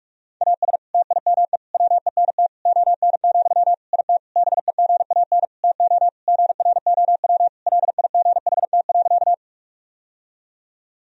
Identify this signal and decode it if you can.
Morse code, heard as a keyed tone, about 46 wpm, 700 Hz. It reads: AS TIME WENT ON, IT BEGAN TO GROW LIGHT.
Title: Sentences from Top 1000 Words - Review Part 1 - 45wpm